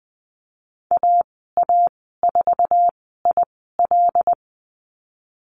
Morse code, heard as a keyed tone, about 20 wpm, spaced as for 15 wpm, 700 Hz. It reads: AA4IL
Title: 082 Lesson 14g - Number 2 - Call Signs 20wpm